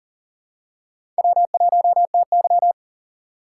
Morse code, heard as a keyed tone, about 40 wpm, 700 Hz. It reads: W1TY
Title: USA Call Signs - 2x Character Spacing - 40wpm